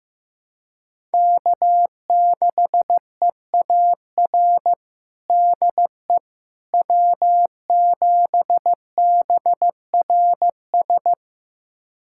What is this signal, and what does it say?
Morse code, heard as a keyed tone, about 15 wpm, 700 Hz. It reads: K6EAR DE W7BRS
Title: USA Callsign DE Callsign - 5 Characters - 15wpm